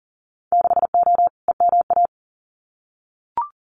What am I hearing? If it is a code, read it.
Morse code, heard as a keyed tone, about 41 wpm, 700 Hz. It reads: 6O EMA
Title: Winter Field Day Exchanges Encore 40wpm